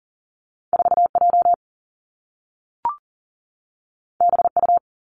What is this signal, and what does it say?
Morse code, heard as a keyed tone, about 41 wpm, 700 Hz. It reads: 4J BU
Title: Two Challenging Letter-Number Combinations 40wpm